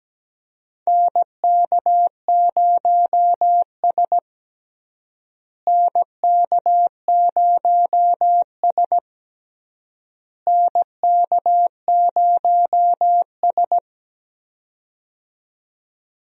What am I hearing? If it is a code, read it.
Morse code, heard as a keyed tone, about 17 wpm, 700 Hz. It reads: NK0S NK0S NK0S
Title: USA Call Signs - 4 Characters - Repeated 3x - 17wpm